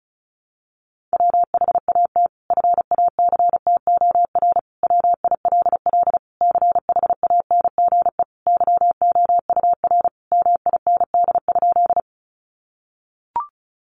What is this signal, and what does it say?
Morse code, heard as a keyed tone, about 34 wpm, 700 Hz. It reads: WHAT FACTOR WILL CHANGE YOUR MIND?
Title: Sentences from Top 1300 Words 35wpm